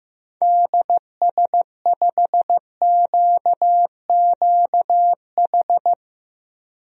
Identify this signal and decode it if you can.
Morse code, heard as a keyed tone, about 15 wpm, 700 Hz. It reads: DS5QQH
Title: DXCC-100 Call Signs - 6 Characters - Repeated 3x - 15wpm